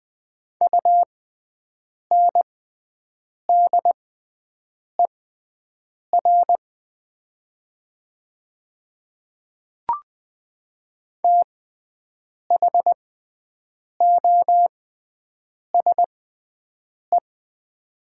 Morse code, heard as a keyed tone, about 20 wpm, spaced as for 7 wpm, 700 Hz. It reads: UNDER THOSE
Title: 6x Character Spacing - Top 300 Words - 20wpm